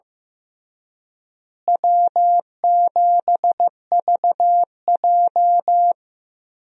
Morse code, heard as a keyed tone, about 15 wpm, 700 Hz. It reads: W7VJ
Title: USA Call Signs - 4 Characters - Repeated 2x - 15wpm